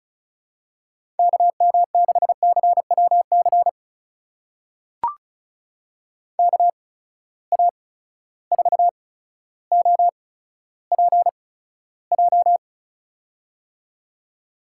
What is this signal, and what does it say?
Morse code, heard as a keyed tone, about 34 wpm, spaced as for 15 wpm, 700 Hz. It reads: KM6CWC KA4OPJ
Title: USA Call Signs - 6 Characters - 8x Character Spacing - 35wpm